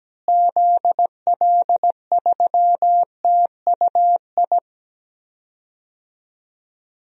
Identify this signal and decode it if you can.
Morse code, heard as a keyed tone, about 17 wpm, 700 Hz. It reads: ZL3TUI
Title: DXCC-100 Call Signs - 6 Characters - Repeated 2x - 17wpm